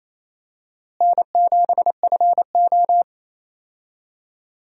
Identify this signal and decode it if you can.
Morse code, heard as a keyed tone, about 28 wpm, 700 Hz. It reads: N7FO
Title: USA Call Signs - 4 Characters - Repeated 3x - 28wpm